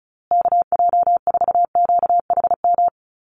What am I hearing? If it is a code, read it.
Morse code, heard as a keyed tone, about 34 wpm, 700 Hz. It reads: KJ4QHM
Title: USA Call Signs Repeated 2x 35wpm